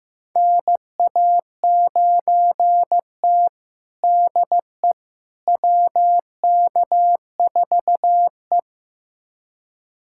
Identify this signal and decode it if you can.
Morse code, heard as a keyed tone, about 16 wpm, 700 Hz. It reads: NA9T DE WK4E
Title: USA Callsign DE Callsign - 4 Characters - 15wpm